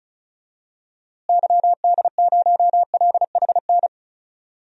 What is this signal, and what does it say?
Morse code, heard as a keyed tone, about 34 wpm, 700 Hz. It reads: YD0LHN
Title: DXCC-100 Call Signs - 6 Characters - Repeated 2x - 35wpm